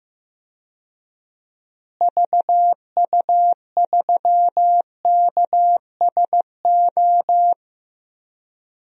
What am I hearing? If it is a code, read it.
Morse code, heard as a keyed tone, about 15 wpm, 700 Hz. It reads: VU3KSO
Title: DXCC-100 Call Signs - 6 Characters - Repeated 2x - 15wpm